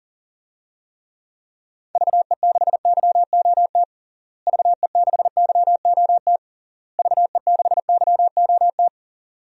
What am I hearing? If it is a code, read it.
Morse code, heard as a keyed tone, about 37 wpm, 700 Hz. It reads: VE6YOT VE6YOT VE6YOT
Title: DXCC-100 Call Signs - 6 Characters - Repeated 3x - 40wpm